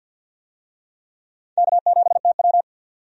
Morse code, heard as a keyed tone, about 53 wpm, 700 Hz. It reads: K7TW
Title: USA Call Signs - 4 Characters - 50wpm